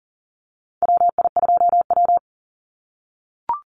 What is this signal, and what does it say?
Morse code, heard as a keyed tone, about 40 wpm, 700 Hz. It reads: WI2W
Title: NA SOTA Chaser Callsigns 40wpm